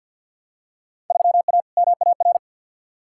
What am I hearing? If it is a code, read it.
Morse code, heard as a keyed tone, about 47 wpm, 700 Hz. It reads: 3A MAR
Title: Field Day Exchanges Encore - 50wpm